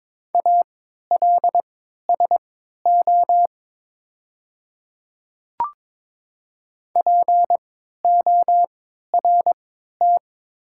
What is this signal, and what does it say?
Morse code, heard as a keyed tone, about 22 wpm, spaced as for 13 wpm, 700 Hz. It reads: ALSO PORT
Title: 3x Character Spacing - Top 200 Words - 22wpm